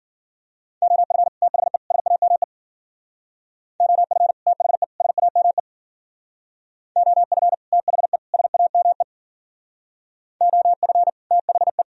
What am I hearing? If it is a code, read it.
Morse code, heard as a keyed tone, about 51 wpm, 700 Hz. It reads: OF THE SAME OF THE SAME OF THE SAME OF THE